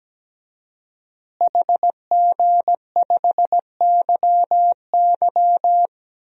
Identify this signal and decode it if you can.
Morse code, heard as a keyed tone, about 17 wpm, 700 Hz. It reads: HG5YY
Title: DXCC-100 Call Signs - 5 Characters - Repeated 3x - 17wpm